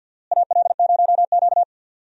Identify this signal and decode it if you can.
Morse code, heard as a keyed tone, about 52 wpm, 700 Hz. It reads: AL0Q
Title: USA Call Signs - 4 Characters - Repeated 3x - 50wpm